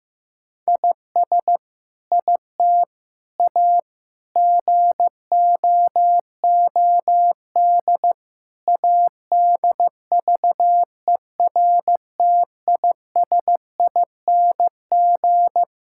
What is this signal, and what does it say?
Morse code, heard as a keyed tone, about 15 wpm, 700 Hz. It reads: IS IT A GOOD ADVERTISING
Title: Sentences from Top 1000 Words - Review Part 6 - 15wpm